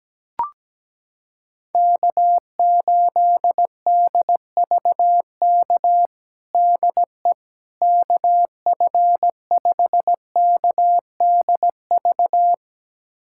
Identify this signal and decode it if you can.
Morse code, heard as a keyed tone, about 17 wpm, 700 Hz. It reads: K8DVK DE KF5KDV
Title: USA Callsign DE Callsign 17wpm